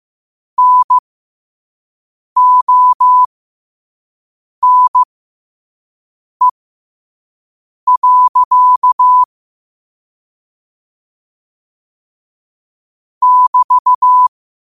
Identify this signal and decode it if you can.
Morse code, heard as a keyed tone, about 15 wpm, spaced as for 5 wpm, 1000 Hz. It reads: NONE. <BT>
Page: New quotes every day in morse code at 5 Words per minute.